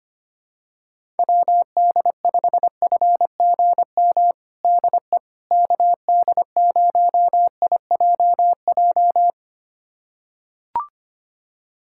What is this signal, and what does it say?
Morse code, heard as a keyed tone, about 25 wpm, 700 Hz. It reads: WD5FGM DE KD0IJJ
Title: USA Callsign DE Callsign - 6 Characters - 25wpm